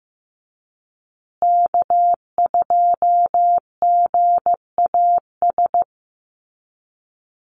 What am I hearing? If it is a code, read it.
Morse code, heard as a keyed tone, about 15 wpm, 700 Hz. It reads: K2GAS
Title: USA Call Signs Repeated 2x 15wpm